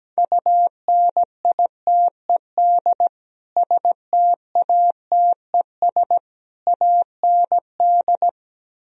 Morse code, heard as a keyed tone, about 17 wpm, 700 Hz. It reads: UNITED STATES AND
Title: Speed-Racing - Sets of 3 Words 17wpm